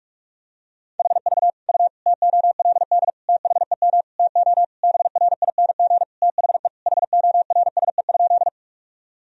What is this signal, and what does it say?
Morse code, heard as a keyed tone, about 46 wpm, 700 Hz. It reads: HV U TOLD THEM TO BRING THE HORSE?
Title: Sentences from Top 500 Words and Abbreviations 45wpm